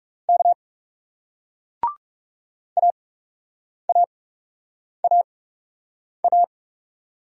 Morse code, heard as a keyed tone, about 40 wpm, spaced as for 8 wpm, 700 Hz. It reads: K AAAA